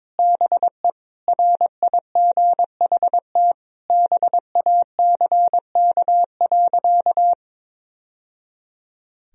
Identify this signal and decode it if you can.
Morse code, heard as a keyed tone, about 22 wpm, 700 Hz. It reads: BE RIGHT BACK.
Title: Sentences from Top 1000 Words - Review Part 4 - 22wpm